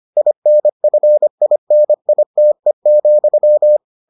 Morse code, heard as a keyed tone, about 25 wpm, 600 Hz. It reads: INFINITE,